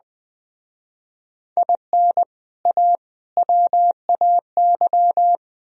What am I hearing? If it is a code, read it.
Morse code, heard as a keyed tone, about 20 wpm, 700 Hz. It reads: IN A WAY
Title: Speed-Racing - Sets of 3 Words 17wpm